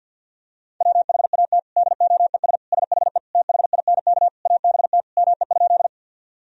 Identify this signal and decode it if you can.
Morse code, heard as a keyed tone, about 50 wpm, 700 Hz. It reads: WHAT DOES SHE THINK ABT ME?
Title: Sentences from Top 200 Words and Abbreviations 50wpm